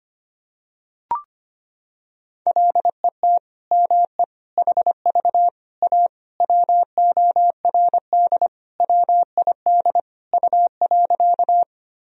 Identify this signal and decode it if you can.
Morse code, heard as a keyed tone, about 25 wpm, 700 Hz. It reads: LET ME HV A WORD WID U.
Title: Sentences from Top 300 Words and Abbreviations 25wpm